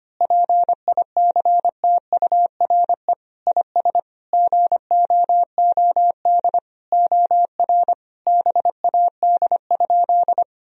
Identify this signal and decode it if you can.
Morse code, heard as a keyed tone, about 25 wpm, 700 Hz. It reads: PICTURE IS GOOD OR BAD?
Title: Sentences from Top 1000 Words - Review Part 1 - 25wpm